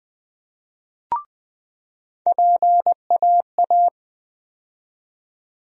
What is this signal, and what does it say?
Morse code, heard as a keyed tone, about 21 wpm, 700 Hz. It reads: PAA
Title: USA Call Sign Postfixes - 3 Letters 20wpm